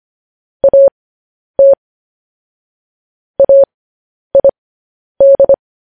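Morse code, heard as a keyed tone, about 25 wpm, spaced as for 10 wpm, 550 Hz. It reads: AT AID
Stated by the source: Morse code news headlines broadcast at 10 wpm.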